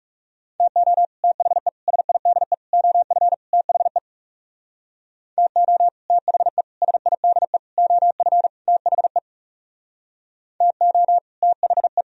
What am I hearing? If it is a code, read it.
Morse code, heard as a keyed tone, about 38 wpm, 700 Hz. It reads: TO THE SIDE OF THE TO THE SIDE OF THE TO THE